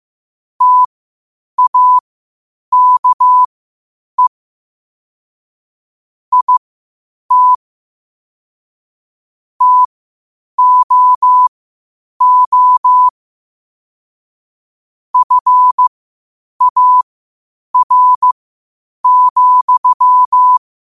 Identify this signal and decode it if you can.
Morse code, heard as a keyed tone, about 15 wpm, spaced as for 8 wpm, 1000 Hz. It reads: TAKE IT TOO FAR,
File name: qotdfiles / 8wpm